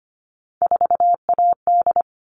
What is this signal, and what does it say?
Morse code, heard as a keyed tone, about 25 wpm, 700 Hz. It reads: 4AD
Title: All Cut Numbers - 3-Digit 25wpm